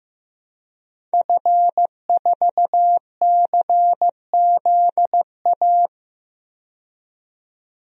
Morse code, heard as a keyed tone, about 15 wpm, 700 Hz. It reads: F4CZA
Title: DXCC-100 Call Signs - 5 Characters - 15wpm